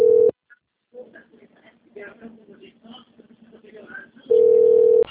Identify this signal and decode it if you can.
Ringback tone (North America), 440 + 480 Hz, with speech.
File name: out- - - - - .13366